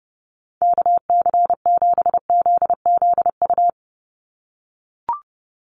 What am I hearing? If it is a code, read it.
Morse code, heard as a keyed tone, about 30 wpm, 700 Hz. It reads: KC7ZZU
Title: USA Call Signs 30wpm